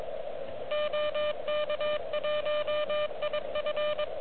Morse code, heard as a keyed tone, about 23 wpm, 590 Hz, with noise. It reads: OK1IF
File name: OK QRP 2022 / Zvuk / REC36